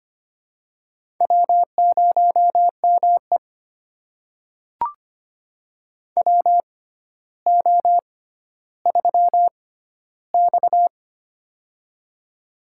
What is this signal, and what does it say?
Morse code, heard as a keyed tone, about 25 wpm, spaced as for 13 wpm, 700 Hz. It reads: W0ME WO3X
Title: USA Call Signs - 4 Characters - 6x Character Spacing - 25wpm